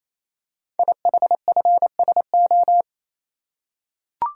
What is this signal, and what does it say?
Morse code, heard as a keyed tone, about 29 wpm, 700 Hz. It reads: IHFSO
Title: Code Groups - 5 Letters - 28wpm